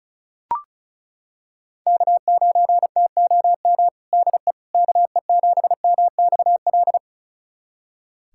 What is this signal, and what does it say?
Morse code, heard as a keyed tone, about 36 wpm, 700 Hz. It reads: K9TOM DE KE7MXL
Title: USA Callsign DE Callsign 35wpm